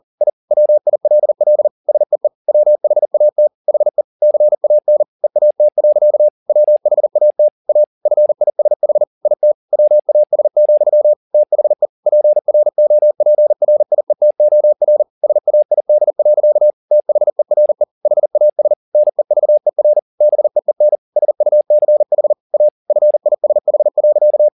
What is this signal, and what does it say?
Morse code, heard as a keyed tone, about 40 wpm, 600 Hz. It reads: I WILL SEE WHAT HE CAN EAT. WHAT A FISH IT WAS, THE PROPRIETOR SAID. THERE HAS NEVER BEEN SUCH A FISH.
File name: the old man and the sea 2 40wpm